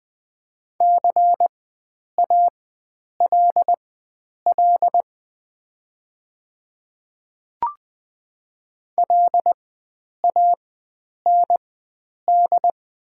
Morse code, heard as a keyed tone, about 20 wpm, spaced as for 9 wpm, 700 Hz. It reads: CALL LAND
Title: 4x Character Spacing - Top 300 Words - 20wpm